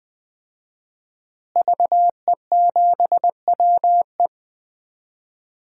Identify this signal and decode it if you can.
Morse code, heard as a keyed tone, about 21 wpm, 700 Hz. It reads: VE7WE
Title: DXCC-100 Call Signs - 5 Characters - Repeated 3x - 20wpm